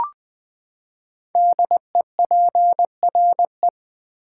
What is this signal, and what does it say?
Morse code, heard as a keyed tone, about 20 wpm, 700 Hz. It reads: DEPRE
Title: Build A Word - 5 Letters At A Time - 10-Letter Words - 20wpm